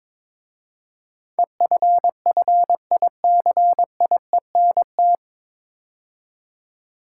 Morse code, heard as a keyed tone, about 22 wpm, 700 Hz. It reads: EFFICIENT
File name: 9 Letter Words - First 3 Letters Given - 22wpm